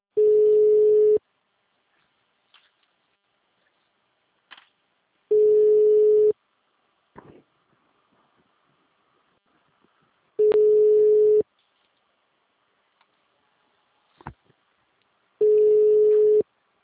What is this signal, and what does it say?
Ringback tone (Europe, 425 Hz).
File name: out- -2051- - - .72212